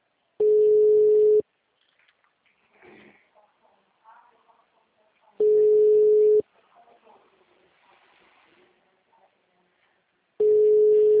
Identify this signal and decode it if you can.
Ringback tone (Europe, 425 Hz), with speech.